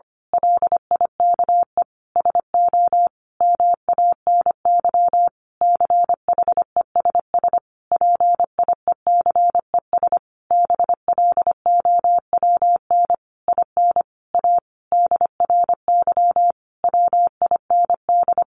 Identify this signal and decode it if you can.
Morse code, heard as a keyed tone, about 25 wpm, 700 Hz. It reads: LIKE SO MANY CHESS PIECES BLOWN IN A DRY WIND
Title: A Sound Of Thunder 25wpm